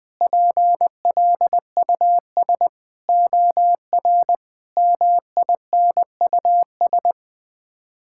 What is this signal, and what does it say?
Morse code, heard as a keyed tone, about 20 wpm, 700 Hz. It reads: PLUS OR MINUS
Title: Sets of 8 Words - 20wpm